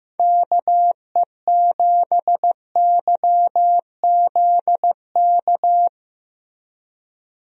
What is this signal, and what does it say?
Morse code, heard as a keyed tone, about 15 wpm, 700 Hz. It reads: KE7YZK
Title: USA Call Signs - 6 Characters - Repeated 2x - 15wpm